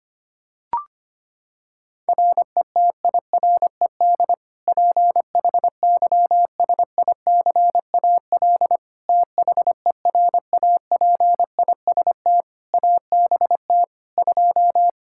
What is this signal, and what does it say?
Morse code, heard as a keyed tone, about 25 wpm, 700 Hz. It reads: RETIRED PHYSICAL THERAPIST ABT 2
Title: K1USN SST QSO Practice 25wpm